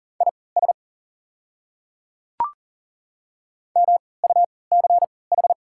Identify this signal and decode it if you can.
Morse code, heard as a keyed tone, about 40 wpm, spaced as for 23 wpm, 700 Hz. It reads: IS MUCH
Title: 3x Character Spacing - Top 300 Words - 40wpm